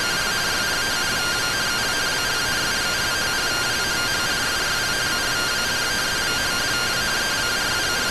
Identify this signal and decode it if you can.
Two-tone teleprinter signal, about 170 Hz shift.